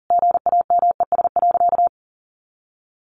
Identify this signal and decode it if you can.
Morse code, heard as a keyed tone, about 40 wpm, 700 Hz. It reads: GAMES.
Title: Additional Easy Phrases 40wpm